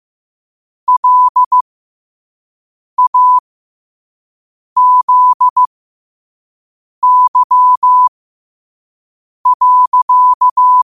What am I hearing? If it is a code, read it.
Morse code, heard as a keyed tone, about 15 wpm, spaced as for 5 wpm, 1000 Hz. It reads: LAZY.
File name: 5wpm